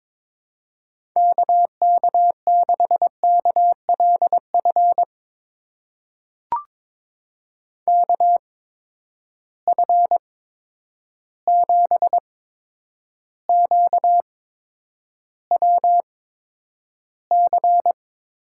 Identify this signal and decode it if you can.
Morse code, heard as a keyed tone, about 22 wpm, spaced as for 9 wpm, 700 Hz. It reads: KK6KLF KF7QWC
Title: USA Call Signs - 6 Characters - 8x Character Spacing - 22wpm